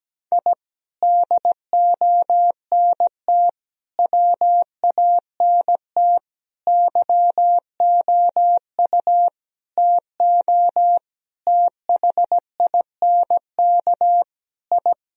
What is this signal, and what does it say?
Morse code, heard as a keyed tone, about 17 wpm, 700 Hz. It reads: I DONT WANT YOU TO THINK I
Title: Sentences from Top 1000 Words - Review Part 3 - 17wpm